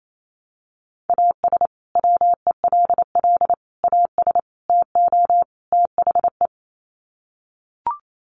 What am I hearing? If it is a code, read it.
Morse code, heard as a keyed tone, about 28 wpm, 700 Hz. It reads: AS WELL AS TO THE
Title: Sets of 5 Words Encore2 28wpm